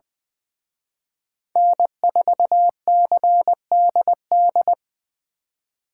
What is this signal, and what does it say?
Morse code, heard as a keyed tone, about 24 wpm, 700 Hz. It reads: N4CDD
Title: Speed Racing - USA 1x3 Call Signs 20wpm